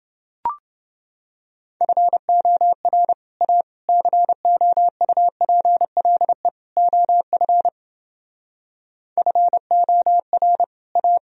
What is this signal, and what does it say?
Morse code, heard as a keyed tone, about 32 wpm, 700 Hz. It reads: FOR A COUPLE OF FOR A